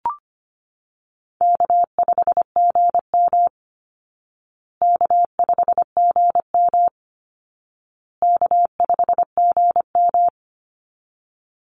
Morse code, heard as a keyed tone, about 25 wpm, 700 Hz. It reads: K5GM K5GM K5GM
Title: Easy Call Signs Repeated 3x - 25wpm